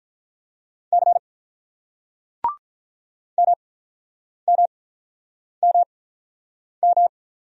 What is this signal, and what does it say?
Morse code, heard as a keyed tone, about 48 wpm, spaced as for 9 wpm, 690 Hz. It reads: C MMMM